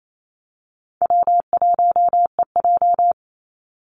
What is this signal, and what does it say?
Morse code, heard as a keyed tone, about 28 wpm, 700 Hz. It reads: W1EJ
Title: NA SOTA Activator Callsigns 28wpm